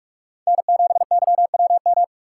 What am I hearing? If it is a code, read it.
Morse code, heard as a keyed tone, about 46 wpm, 700 Hz. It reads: N7YWM
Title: USA Call Signs - 5 Characters - 3x Character Spacing - 45wpm